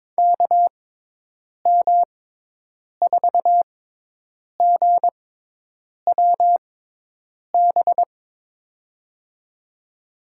Morse code, heard as a keyed tone, about 22 wpm, spaced as for 8 wpm, 700 Hz. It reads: KM4GWB
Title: USA Call Signs - 6x Character Spacing - 22wpm